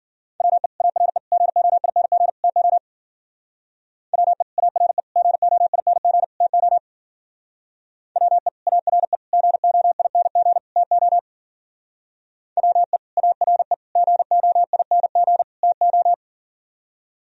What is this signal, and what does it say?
Morse code, heard as a keyed tone, about 52 wpm, 700 Hz. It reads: WE ARE GOING TO WE ARE GOING TO WE ARE GOING TO WE ARE GOING TO